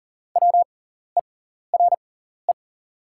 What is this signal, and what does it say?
Morse code, heard as a keyed tone, about 40 wpm, spaced as for 14 wpm, 700 Hz. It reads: WERE
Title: 6x Character Spacing - Top 300 Words - 40wpm